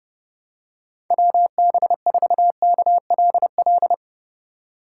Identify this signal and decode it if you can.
Morse code, heard as a keyed tone, about 30 wpm, 700 Hz. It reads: WB4KLL
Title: USA Call Signs - 6 Characters - 4x Character Spacing - 30wpm